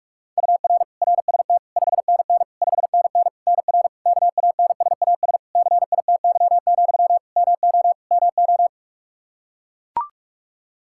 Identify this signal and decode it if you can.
Morse code, heard as a keyed tone, about 41 wpm, 700 Hz. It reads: UR RST 5NN 5NN NR KANSAS CITY, MO MO
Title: Practice QSO - Exchange 1 - Beginning - 45wpm